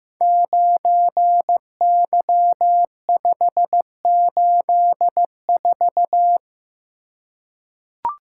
Morse code, heard as a keyed tone, about 15 wpm, 700 Hz. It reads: 9Y584
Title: Code Groups - 5 Letter:Numbers - 15wpm